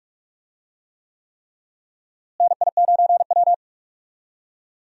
Morse code, heard as a keyed tone, about 42 wpm, 700 Hz. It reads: NI9W
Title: USA Call Signs - 4 Characters - Repeated 3x - 45wpm